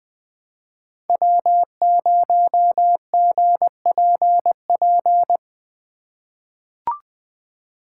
Morse code, heard as a keyed tone, about 20 wpm, 700 Hz. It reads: W0GPP
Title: USA Call Signs - 5 Characters - 2x Character Spacing - 20wpm